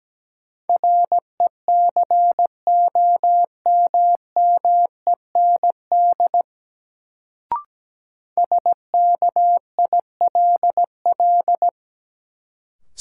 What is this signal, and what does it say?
Morse code, heard as a keyed tone, about 17 wpm, 700 Hz. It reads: RECOMMEND SKILL
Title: Sentences from Top 1200 Words - New Word Drill - 17wpm